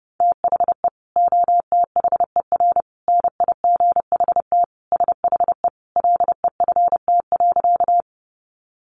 Morse code, heard as a keyed tone, about 30 wpm, 700 Hz. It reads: THE OTHER NIGHT SHE LEFT.
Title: Sentences from Top 300 Words 30wpm